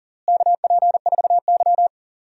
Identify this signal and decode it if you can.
Morse code, heard as a keyed tone, about 39 wpm, 700 Hz. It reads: KP4Y
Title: DXCC-100 Call Signs - 4 Characters - Repeated 2x - 40wpm